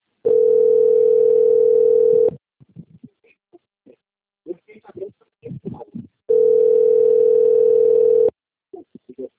Ringback tone (North America), 440 + 480 Hz.
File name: out- - - - - .4377